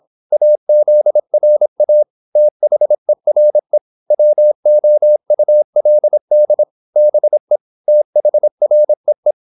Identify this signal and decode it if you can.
Morse code, heard as a keyed tone, about 26 wpm, 600 Hz. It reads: AZRA THERE WOULD BE THREE
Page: Never 26wpm | CW med Gnister